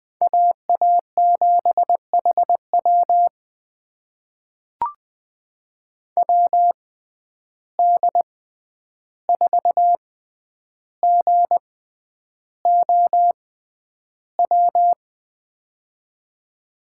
Morse code, heard as a keyed tone, about 21 wpm, spaced as for 10 wpm, 700 Hz. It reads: AA7HW WD4GOW
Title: USA Call Signs - 6x Character Spacing - 20wpm